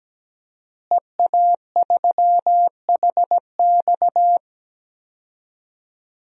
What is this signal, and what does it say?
Morse code, heard as a keyed tone, about 17 wpm, 700 Hz. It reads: EA3HX
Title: DXCC-100 Call Signs - 5 Characters - Repeated 3x - 17wpm